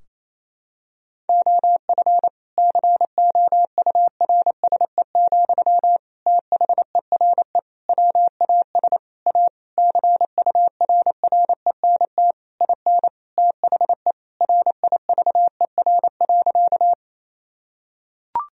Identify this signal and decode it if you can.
Morse code, heard as a keyed tone, about 28 wpm, 700 Hz. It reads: OF COURSE, THERE WAS A CURRENT IN THE RIVER.
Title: Sentences from Top 1000 Words - Review Part 4 - 28wpm